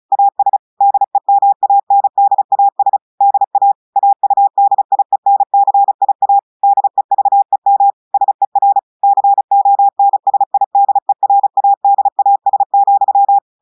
Morse code, heard as a keyed tone, about 36 wpm, 800 Hz. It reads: AS DEMANDAS DA AUDIENCIA DEVEM SER CONSIDERADAS,
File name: audios / portugais / portugais texte en clair 2 vitesse 35 mots minutes